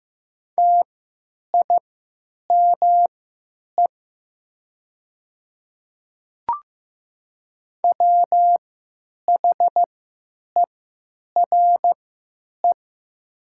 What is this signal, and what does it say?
Morse code, heard as a keyed tone, about 15 wpm, spaced as for 9 wpm, 700 Hz. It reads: TIME WHERE
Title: 3x Character Spacing - Top 200 Words - 15wpm